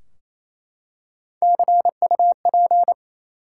Morse code, heard as a keyed tone, about 28 wpm, 700 Hz. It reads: CUP
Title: Three Letter Words - 28wpm